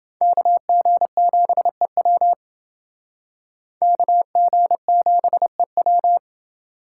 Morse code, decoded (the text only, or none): KG7EW KG7EW